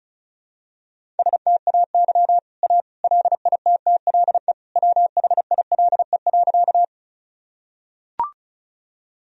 Morse code, heard as a keyed tone, about 36 wpm, 700 Hz. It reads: STAY A LITTLE WHILE.
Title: Sentences from Top 1000 Words - Review Part 1 - 35wpm